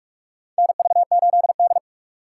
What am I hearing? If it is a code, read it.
Morse code, heard as a keyed tone, about 46 wpm, 700 Hz. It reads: NV8D
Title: USA Call Signs - 4 Characters - 8x Character Spacing - 45wpm